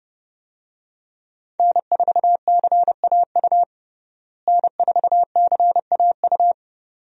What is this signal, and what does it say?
Morse code, heard as a keyed tone, about 30 wpm, 700 Hz. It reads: N4CAU N4CAU
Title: USA Call Signs - 5 Characters - Repeated 2x - 30wpm